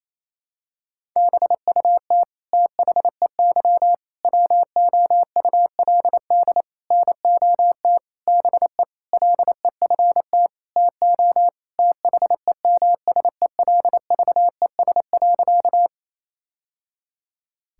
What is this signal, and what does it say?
Morse code, heard as a keyed tone, about 28 wpm, 700 Hz. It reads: BUT THEY WOULD NOT BE LEFT TO THEMSELVES.
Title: Sentences from Top 1000 Words - Review Part 5 - 28wpm